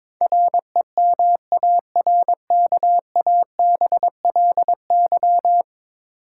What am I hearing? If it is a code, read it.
Morse code, heard as a keyed tone, about 21 wpm, 700 Hz. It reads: REMARKABLY
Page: morse-code-ninja-practice-22wpm